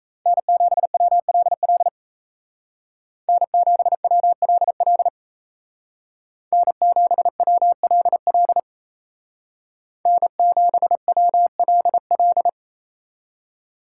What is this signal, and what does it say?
Morse code, heard as a keyed tone, about 33 wpm, 700 Hz. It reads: N7WLL N7WLL N7WLL N7WLL
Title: Speed Racing - USA 1x3 Call Signs 28wpm